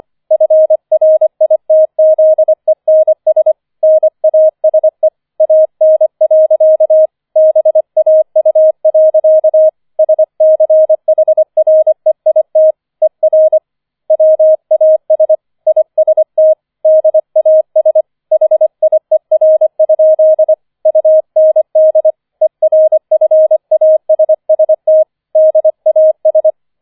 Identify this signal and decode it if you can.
Morse code, heard as a keyed tone, about 25 wpm, 630 Hz. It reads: FRITZENS NASE AN. BAU. SCHREIT ER WAS IST DAS HIER? UND ERFASST DAS
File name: 005 Fünfter Streich 25wpm